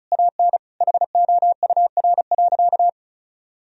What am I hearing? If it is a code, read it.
Morse code, heard as a keyed tone, about 36 wpm, 700 Hz. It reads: AN HOUR.
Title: Sentences from Top 1000 Words - Review Part 4 - 35wpm